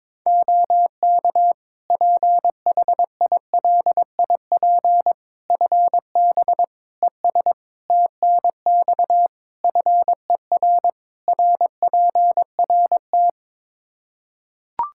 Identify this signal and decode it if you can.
Morse code, heard as a keyed tone, about 22 wpm, 700 Hz. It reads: OK PHILIP FB ES TNX FER RPRT
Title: K1USN SST QSO Practice 22wpm